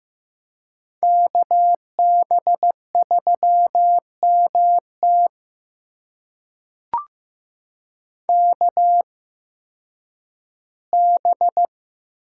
Morse code, heard as a keyed tone, about 15 wpm, spaced as for 10 wpm, 700 Hz. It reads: KB3MT KB
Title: USA Call Signs - 8x Character Spacing - 15wpm